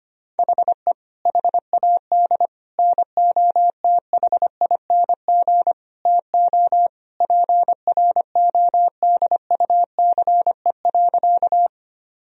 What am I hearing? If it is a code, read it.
Morse code, heard as a keyed tone, about 25 wpm, 700 Hz. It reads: HE HAD NOTHING TO PRODUCE.
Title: Sentences from Top 1000 Words - Review Part 3 - 25wpm